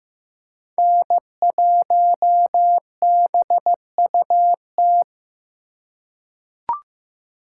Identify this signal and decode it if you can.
Morse code, heard as a keyed tone, about 15 wpm, 700 Hz. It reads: N1BUT
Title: USA Call Signs - 3x Character Spacing - 15wpm